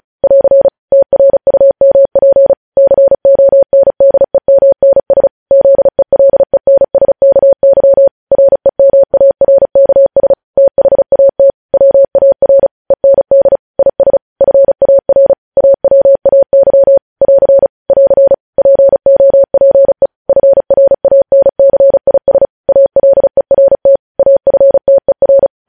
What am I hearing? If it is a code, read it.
Morse code, heard as a keyed tone, about 36 wpm, 550 Hz. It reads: <AR> TRUMP CONDEMNS ZELENSKY REMARKS THAT WAR END IS FAR AWAY <AR> <AR> POPE FRANCIS ALERT AFTER
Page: News Headlines in Morse Code at 35 WPM